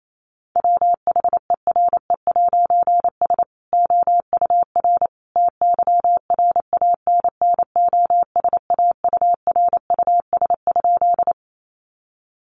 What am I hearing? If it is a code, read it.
Morse code, heard as a keyed tone, about 28 wpm, 700 Hz. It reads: WHERE'S OUR TYRANNOSAURUS?
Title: A Sound Of Thunder 28wpm